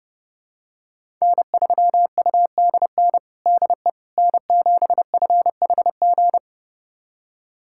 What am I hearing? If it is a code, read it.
Morse code, heard as a keyed tone, about 32 wpm, 700 Hz. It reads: N3UDN DE N7FHG
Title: USA Callsign DE Callsign - 5 Characters - 30wpm